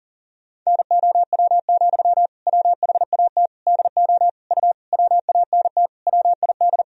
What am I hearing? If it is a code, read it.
Morse code, heard as a keyed tone, about 37 wpm, 700 Hz. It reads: NOW, WHAT DO U WANT WID
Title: Sentences from Top 200 Words and Abbreviations 40wpm